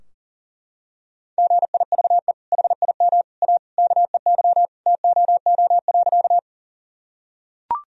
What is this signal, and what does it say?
Morse code, heard as a keyed tone, about 40 wpm, 700 Hz. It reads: GIVE HIM A KEY TOO.
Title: Sentences from Top 1000 Words - Review Part 4 - 40wpm